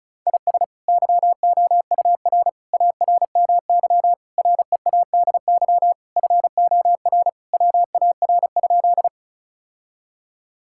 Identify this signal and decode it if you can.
Morse code, heard as a keyed tone, about 36 wpm, 700 Hz. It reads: IS YOUR ARMY READY FOR WAR?
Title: Sentences from Top 1000 Words - Review Part 5 - 35wpm